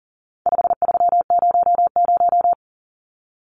Morse code, heard as a keyed tone, about 40 wpm, 700 Hz. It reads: 5300
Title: Numbers - 4-Digit 40wpm